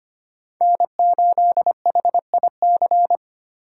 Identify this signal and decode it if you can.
Morse code, heard as a keyed tone, about 25 wpm, 700 Hz. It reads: N8HIC
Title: USA Call Signs - 5 Characters - 8x Character Spacing - 25wpm